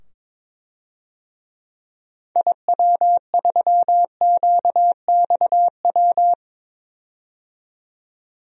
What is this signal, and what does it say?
Morse code, heard as a keyed tone, about 21 wpm, 700 Hz. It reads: IW3QXW
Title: DXCC-100 Call Signs - 6 Characters - 22wpm